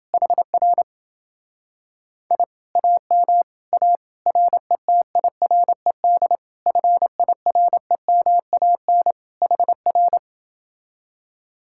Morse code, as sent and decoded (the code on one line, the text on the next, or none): .... .-. / .. / .- -- / .- / .-. - .. .-. -.. / ..-. .. .-. -- .- -. / .... .-.
HR I AM A RETIRED FIREMAN HR